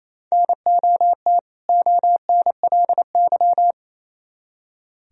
Morse code, heard as a keyed tone, about 34 wpm, 700 Hz. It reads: NOT ONLY
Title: Speed-Racing - Sets of 2 Words 28wpm